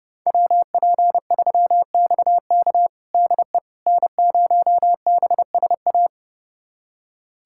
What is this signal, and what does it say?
Morse code, heard as a keyed tone, about 30 wpm, 700 Hz. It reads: WP3XK DE N0BSA
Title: USA Callsign DE Callsign - 5 Characters - 30wpm